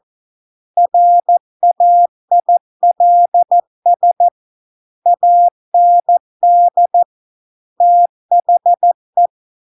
Morse code, heard as a keyed tone, about 14 wpm, 700 Hz. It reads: RAILS AND THE